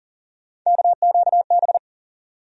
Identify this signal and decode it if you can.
Morse code, heard as a keyed tone, about 38 wpm, 700 Hz. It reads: KQB
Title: USA Call Sign Postfixes - 3 Letters 40wpm